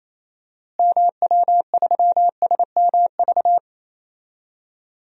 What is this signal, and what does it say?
Morse code, heard as a keyed tone, about 28 wpm, 700 Hz. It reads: MW3SMV
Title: DXCC-100 Call Signs - 6 Characters - Repeated 2x - 28wpm